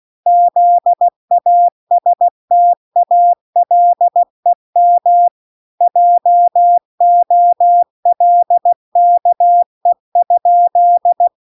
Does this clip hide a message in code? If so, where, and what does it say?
Yes: Morse code from 0.3 to 11.3 s, heard as a keyed tone, about 16 wpm, 700 Hz. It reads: ZASTALEM JOLKE?